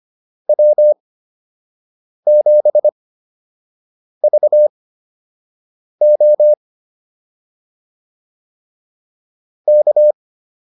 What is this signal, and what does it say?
Morse code, heard as a keyed tone, about 25 wpm, spaced as for 6 wpm, 600 Hz. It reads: W7VO K
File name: QSO 105 06